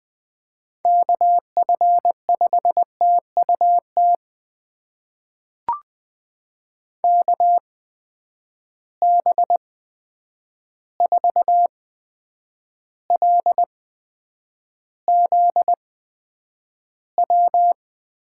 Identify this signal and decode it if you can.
Morse code, heard as a keyed tone, about 20 wpm, spaced as for 9 wpm, 700 Hz. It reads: KF5TUT KB4LZW
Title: USA Call Signs - 6 Characters - 8x Character Spacing - 20wpm